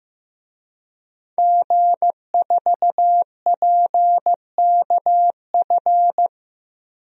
Morse code, heard as a keyed tone, about 15 wpm, 700 Hz. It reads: G4PKF
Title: DXCC-100 Call Signs - 5 Characters - 15wpm